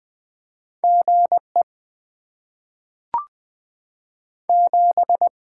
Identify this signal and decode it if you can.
Morse code, heard as a keyed tone, about 20 wpm, 700 Hz. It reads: G E 7
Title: Top 20 QSO Elements - 8x Character Spacing - 20wpm